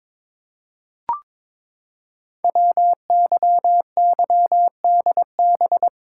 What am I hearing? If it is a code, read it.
Morse code, heard as a keyed tone, about 23 wpm, 700 Hz. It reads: WYYDB
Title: Code Groups - 5 Letters - 22wpm